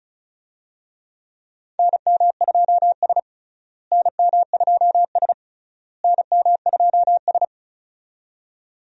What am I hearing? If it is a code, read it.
Morse code, heard as a keyed tone, about 34 wpm, 700 Hz. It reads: NM2S NM2S NM2S
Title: USA Call Signs - 4 Characters - Repeated 3x - 35wpm